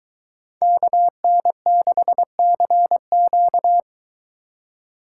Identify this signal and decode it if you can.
Morse code, heard as a keyed tone, about 23 wpm, 700 Hz. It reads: KN6CQ